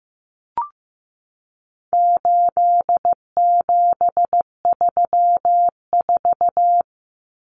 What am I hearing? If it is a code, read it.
Morse code, heard as a keyed tone, about 15 wpm, 700 Hz. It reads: 8734
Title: Common Cut Numbers - 4-Digit 15wpm